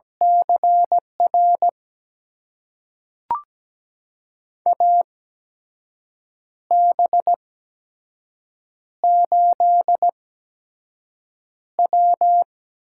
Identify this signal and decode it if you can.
Morse code, heard as a keyed tone, about 17 wpm, spaced as for 6 wpm, 700 Hz. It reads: CR AB8W
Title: USA Call Signs - 8x Character Spacing - 17wpm